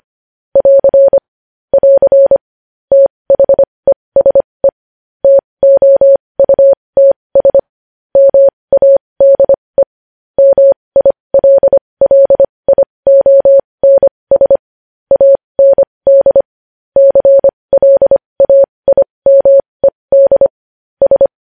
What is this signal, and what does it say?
Morse code, heard as a keyed tone, about 25 wpm, spaced as for 20 wpm, 550 Hz. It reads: <AR> <AR> THESE TOUTS MADE MILLIONS AND CLAIMED S
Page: News headlines in Morse code at 20 wpm